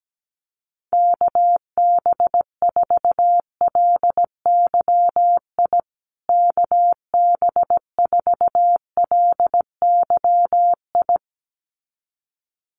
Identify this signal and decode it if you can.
Morse code, heard as a keyed tone, about 17 wpm, 700 Hz. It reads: KB4LYI KB4LYI
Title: USA Call Signs Repeated 2x 17wpm